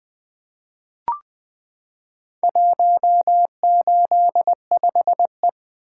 Morse code, heard as a keyed tone, about 20 wpm, 700 Hz. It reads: 185E
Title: Four Letter-Number Combinations - 20wpm